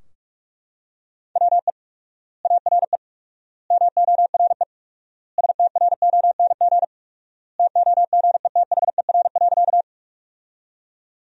Morse code, heard as a keyed tone, about 41 wpm, 700 Hz. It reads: WE ARE MORE STRONG TOGETHER.
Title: 4x Word-Spacing - Sentences from Top 500 Words - 45wpm